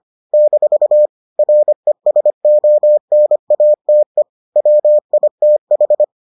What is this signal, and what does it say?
Morse code, heard as a keyed tone, about 25 wpm, 600 Hz. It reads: - RESONATE WITH